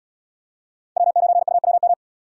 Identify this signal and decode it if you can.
Morse code, heard as a keyed tone, about 70 wpm, 700 Hz. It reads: W9UWM